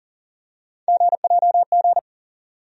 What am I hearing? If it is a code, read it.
Morse code, heard as a keyed tone, about 40 wpm, 700 Hz. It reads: GJG
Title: USA Call Sign Postfixes - 3 Letters 40wpm